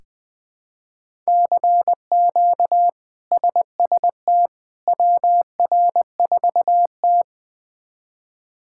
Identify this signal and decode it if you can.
Morse code, heard as a keyed tone, about 20 wpm, 700 Hz. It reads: CQ SST WR4T
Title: K1USN SST QSO Practice - 20wpm